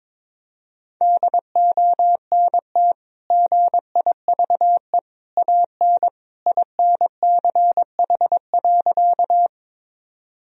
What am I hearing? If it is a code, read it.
Morse code, heard as a keyed tone, about 22 wpm, 700 Hz. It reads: DONT GIVE AN INCH.
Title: Sentences from Top 1000 Words - Review Part 3 - 22wpm